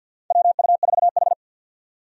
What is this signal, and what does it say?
Morse code, heard as a keyed tone, about 47 wpm, 700 Hz. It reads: WU4H
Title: USA Call Signs - 4 Characters - 50wpm